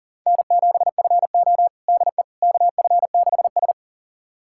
Morse code, heard as a keyed tone, about 37 wpm, 700 Hz. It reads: N7FO DE KF6S
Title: USA Callsign DE Callsign - 4 Characters - 40wpm